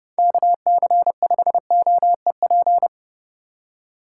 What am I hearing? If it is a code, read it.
Morse code, heard as a keyed tone, about 40 wpm, 700 Hz. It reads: KC5OEP